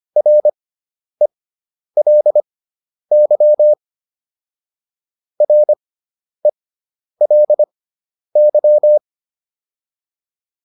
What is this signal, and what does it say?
Morse code, heard as a keyed tone, about 25 wpm, spaced as for 10 wpm, 600 Hz. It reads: RELY RELY